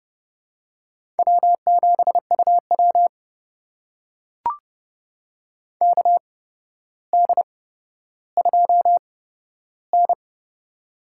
Morse code, heard as a keyed tone, about 30 wpm, spaced as for 13 wpm, 700 Hz. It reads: W7UW KD2N
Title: USA Call Signs - 4 Characters - 8x Character Spacing - 30wpm